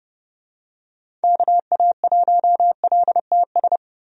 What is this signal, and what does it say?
Morse code, heard as a keyed tone, about 30 wpm, 700 Hz. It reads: KA1LTS
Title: USA Call Signs - 6 Characters - 30wpm